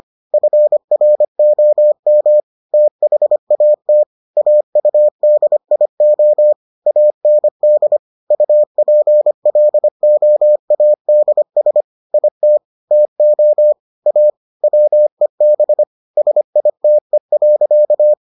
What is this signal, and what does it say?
Morse code, heard as a keyed tone, about 25 wpm, 600 Hz. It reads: FROM THAT AUDIO AND UPLOADS IT TO A WEB SITE.